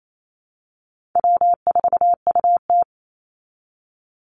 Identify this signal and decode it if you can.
Morse code, heard as a keyed tone, about 28 wpm, 700 Hz. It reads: W4UT
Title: Easy Call Signs Repeated 2x - 28wpm